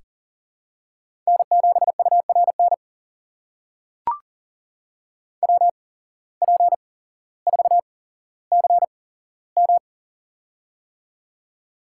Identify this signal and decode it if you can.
Morse code, heard as a keyed tone, about 40 wpm, spaced as for 17 wpm, 700 Hz. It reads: N7URN WP4CM
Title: USA Call Signs - 5 Characters - 8x Character Spacing - 40wpm